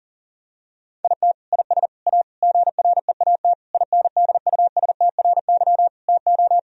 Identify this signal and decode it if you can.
Morse code, heard as a keyed tone, about 40 wpm, 700 Hz. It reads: IT IS A GREAT INDUSTRY TO
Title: Sentences from Top 1000 Words - Review Part 4 - 40wpm